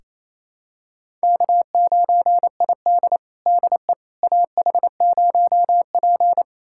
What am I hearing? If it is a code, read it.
Morse code, heard as a keyed tone, about 27 wpm, 700 Hz. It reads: K9ID DE AH0P
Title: USA Callsign DE Callsign - 4 Characters - 28wpm